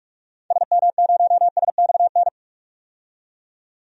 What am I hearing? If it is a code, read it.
Morse code, heard as a keyed tone, about 42 wpm, 700 Hz. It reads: SM0SXN
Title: DXCC-100 Call Signs - 6 Characters - 45wpm